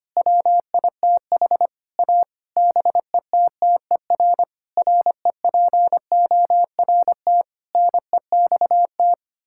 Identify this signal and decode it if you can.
Morse code, heard as a keyed tone, about 25 wpm, 700 Hz. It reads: WITH A BETTER REPORT NEXT
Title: Sentences from Top 1000 Words - Review Part 6 - 25wpm